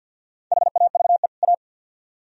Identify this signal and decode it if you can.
Morse code, heard as a keyed tone, about 62 wpm, 700 Hz. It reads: HAVE A
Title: Speed-Racing - Sets of 2 Words 50wpm